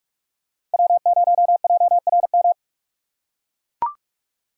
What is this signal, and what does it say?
Morse code, heard as a keyed tone, about 44 wpm, 700 Hz. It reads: W0JRM
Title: USA Call Signs - 5 Characters - 2x Character Spacing - 45wpm